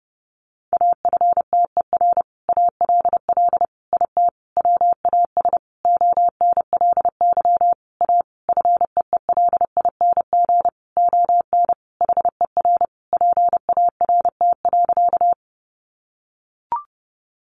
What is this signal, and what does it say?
Morse code, heard as a keyed tone, about 30 wpm, 700 Hz. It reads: AFTER ALL IT WAS ONLY A FEELING ON HER PART.
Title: Sentences from Top 1200 Words 30wpm